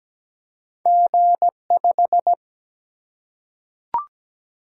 Morse code, heard as a keyed tone, about 17 wpm, 700 Hz. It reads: G5
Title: Single-Letter Single-Number 17wpm